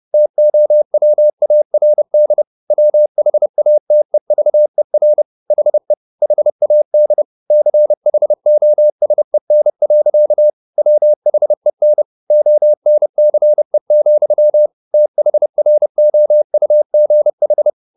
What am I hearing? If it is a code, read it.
Morse code, heard as a keyed tone, about 30 wpm, 600 Hz. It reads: TOWARD WHATEVER HE HAD CHOSEN. WHEN ONCE, THROUGH
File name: the old man and the sea / the old man and the sea 1 30wpm